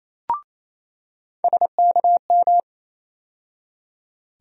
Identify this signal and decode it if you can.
Morse code, heard as a keyed tone, about 28 wpm, 700 Hz. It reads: SKM
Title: USA Call Sign Postfixes - 3 Letters 28wpm